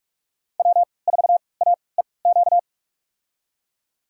Morse code, heard as a keyed tone, about 47 wpm, spaced as for 26 wpm, 700 Hz. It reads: W4AEQ
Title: USA Call Signs - 3x Character Spacing - 45wpm